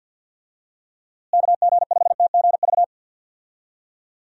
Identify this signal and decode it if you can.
Morse code, heard as a keyed tone, about 51 wpm, 700 Hz. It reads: KG5TGV
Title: USA Call Signs - 6 Characters - Repeated 3x - 50wpm